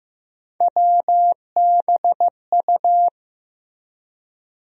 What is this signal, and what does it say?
Morse code, heard as a keyed tone, about 15 wpm, 700 Hz. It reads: WBU
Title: USA Call Sign Postfixes - 3 Letters 15wpm